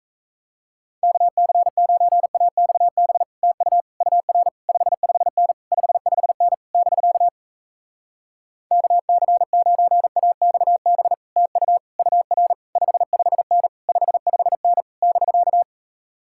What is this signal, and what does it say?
Morse code, heard as a keyed tone, about 37 wpm, 700 Hz. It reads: KC9AXB TU UR 55N 55N <BK> KC9AXB TU UR 55N 55N <BK>